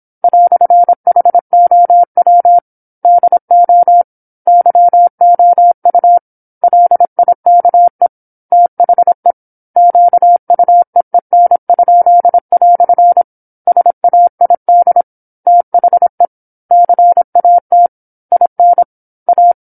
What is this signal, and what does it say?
Morse code, heard as a keyed tone, about 27 wpm, 700 Hz. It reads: "HOW DO YOU LIKE THE QUEEN?" SAID THE CAT IN A
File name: 08-Chapter-08 26 WPM